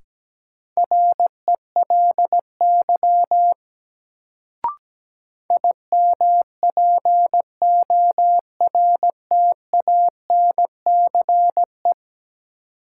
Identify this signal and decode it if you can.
Morse code, heard as a keyed tone, about 17 wpm, 700 Hz. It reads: RELY IMPORTANCE
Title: Sentences from Top 1200 Words - New Word Drill Alt - 17wpm